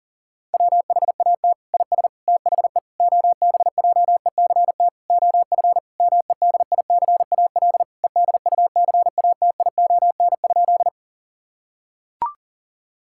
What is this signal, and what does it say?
Morse code, heard as a keyed tone, about 40 wpm, 700 Hz. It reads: WHAT IS THE OBJECT OF MEDICAL EDUCATION?
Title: Sentences from Top 1000 Words - Review Part 6 - 40wpm